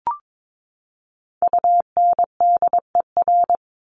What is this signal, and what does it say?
Morse code, heard as a keyed tone, about 22 wpm, 700 Hz. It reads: UNDER
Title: Top 200 Words 22wpm